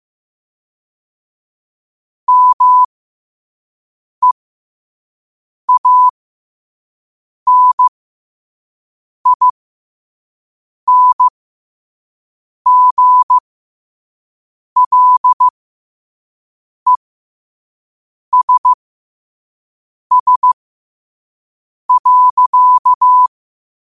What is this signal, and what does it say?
Morse code, heard as a keyed tone, about 15 wpm, spaced as for 5 wpm, 1000 Hz. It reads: MEANINGLESS.
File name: qotdfiles / 5wpm